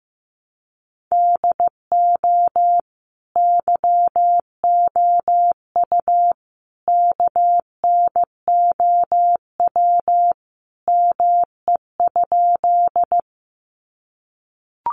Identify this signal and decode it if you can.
Morse code, heard as a keyed tone, about 15 wpm, 700 Hz. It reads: DO YOU KNOW ME?
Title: Sentences from Top 200 Words 15wpm